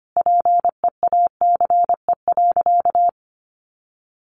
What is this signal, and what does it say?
Morse code, heard as a keyed tone, about 25 wpm, 700 Hz. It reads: PEACE.
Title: Sentences from Top 1500 Words 25wpm